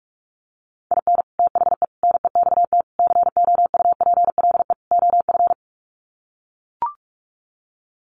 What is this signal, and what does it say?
Morse code, heard as a keyed tone, about 44 wpm, 700 Hz. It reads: IN THE NEXT COUPLE OF
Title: Sets of 5 Words Encore2 45wpm